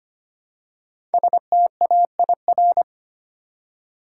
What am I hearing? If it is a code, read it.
Morse code, heard as a keyed tone, about 25 wpm, 700 Hz. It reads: STAIR
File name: 5 Letter Words - First 3 Letters Given - 25wpm